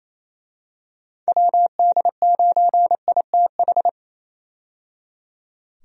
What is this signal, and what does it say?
Morse code, heard as a keyed tone, about 27 wpm, 700 Hz. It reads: WD9ITH
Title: USA Call Signs - 6 Characters - 28wpm